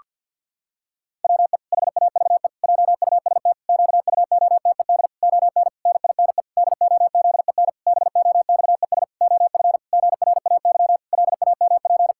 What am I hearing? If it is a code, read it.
Morse code, heard as a keyed tone, about 50 wpm, 700 Hz. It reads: WE HAVE JUST QUOTED ON NINE DOZEN BOXES OF GRAY LAMP
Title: Pangram Challenge v2 - 50wpm